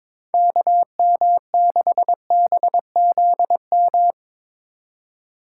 Morse code, heard as a keyed tone, about 22 wpm, 700 Hz. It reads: KM6BZM
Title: USA Call Signs - 6 Characters - 2x Character Spacing - 22wpm